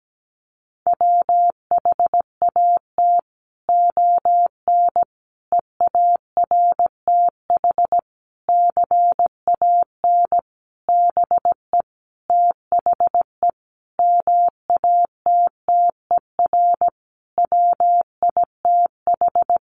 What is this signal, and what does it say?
Morse code, heard as a keyed tone, about 17 wpm, 700 Hz. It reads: WHAT ON EARTH CAN BE THE MATTER WITH
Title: Sentences from Top 700 Words 17wpm